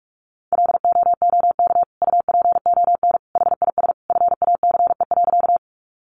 Morse code, heard as a keyed tone, about 44 wpm, 700 Hz. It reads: LOOK UPON HIS FACE.
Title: Sentences from Top 1100 Words 45wpm